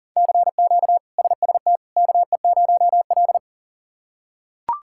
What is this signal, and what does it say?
Morse code, heard as a keyed tone, about 40 wpm, 700 Hz. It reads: CQ SST KE0L
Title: K1USN SST QSO Practice - 40wpm